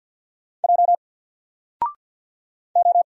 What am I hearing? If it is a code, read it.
Morse code, heard as a keyed tone, about 50 wpm, 700 Hz. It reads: J O
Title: Single Character (Letter:Number:Punctuation:Prosign) 50wpm